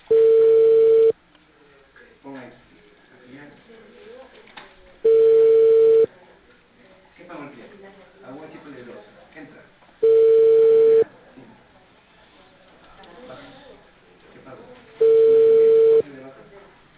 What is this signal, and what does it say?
Ringback tone (China, 450 Hz), with speech.